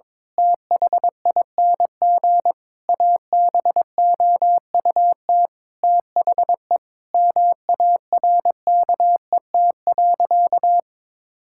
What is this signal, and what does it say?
Morse code, heard as a keyed tone, about 22 wpm, 700 Hz. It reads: THING ABOUT THE MARKET.
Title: Sentences from Top 1000 Words - Review Part 5 - 22wpm